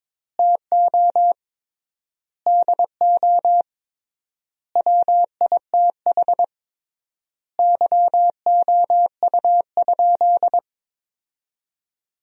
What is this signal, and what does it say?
Morse code, heard as a keyed tone, about 22 wpm, 700 Hz. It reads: TO DO WITH YOU?
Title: 3x Word-Spacing - Sentences from Top 100 Words - 22wpm